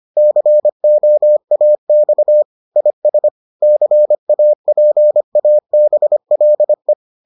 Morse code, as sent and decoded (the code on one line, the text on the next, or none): -.-. --- .- -..- / .. ... / -.-. .- .--. .- -... .-.. .
COAX IS CAPABLE